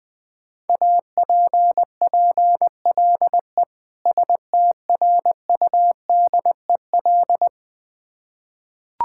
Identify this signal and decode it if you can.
Morse code, heard as a keyed tone, about 21 wpm, 700 Hz. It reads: APPLE STRUDEL
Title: Difficult Words with Hints - 20wpm